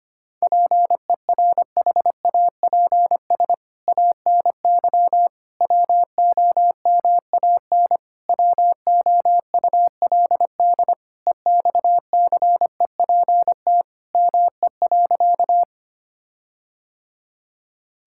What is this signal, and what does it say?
Morse code, heard as a keyed tone, about 25 wpm, 700 Hz. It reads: PERHAPS ANY WOMAN WOULD EXCEPT ME.
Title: Sentences from Top 1600 Words 25wpm